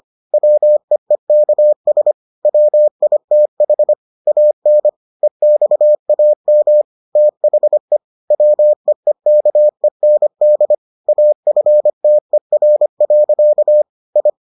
Morse code, heard as a keyed tone, about 25 wpm, 600 Hz. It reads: WEEKS WITH AN EXAM THE WEEKEND AFTER. I